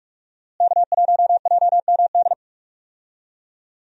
Morse code, heard as a keyed tone, about 45 wpm, 700 Hz. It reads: K1JMD
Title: USA Call Signs - 5 Characters - Repeated 2x - 45wpm